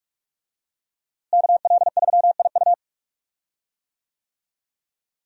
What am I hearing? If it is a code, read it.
Morse code, heard as a keyed tone, about 46 wpm, 700 Hz. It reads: KL3IU
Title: USA Call Signs - 5 Characters - Repeated 3x - 45wpm